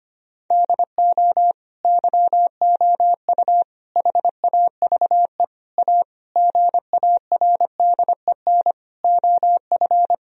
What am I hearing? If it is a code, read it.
Morse code, heard as a keyed tone, about 25 wpm, 700 Hz. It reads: DO YOU HAVE A GARDEN OF
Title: Sentences from Top 1000 Words - Review Part 4 - 25wpm